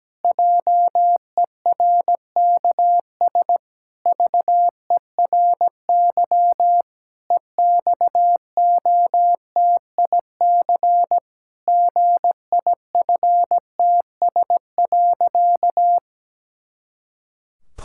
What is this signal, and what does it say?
Morse code, heard as a keyed tone, about 17 wpm, 700 Hz. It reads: JERKS VERY EXOTIC GIFTS.
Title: Pangram Challenge v2 - 17wpm